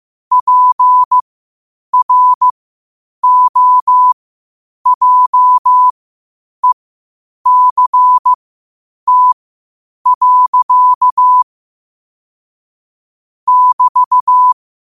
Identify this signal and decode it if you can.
Morse code, heard as a keyed tone, about 15 wpm, spaced as for 8 wpm, 1000 Hz. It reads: PROJECT. <BT>
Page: Quotes for Thu, 14 Aug 2025 in Morse Code at 8 words per minute.